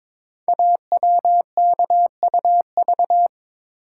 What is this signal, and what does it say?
Morse code, heard as a keyed tone, about 22 wpm, 700 Hz. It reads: AWKUV
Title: Code Groups - 5 Letters - 22wpm